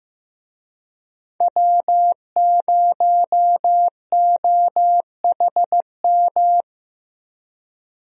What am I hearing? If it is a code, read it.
Morse code, heard as a keyed tone, about 15 wpm, 700 Hz. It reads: W0OHM
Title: USA Call Signs - 5 Characters - Repeated 2x - 15wpm